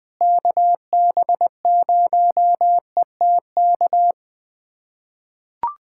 Morse code, heard as a keyed tone, about 20 wpm, 700 Hz. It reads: KB0ETK
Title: USA Callsign DE Callsign 20wpm